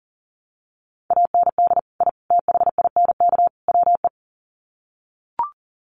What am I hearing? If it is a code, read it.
Morse code, heard as a keyed tone, about 40 wpm, 700 Hz. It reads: AND I THINK WE
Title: Sets of 4 Words 40wpm